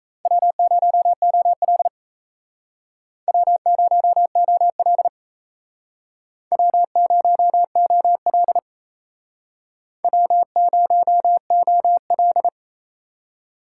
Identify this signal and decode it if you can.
Morse code, heard as a keyed tone, about 36 wpm, 700 Hz. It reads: W0OL W0OL W0OL W0OL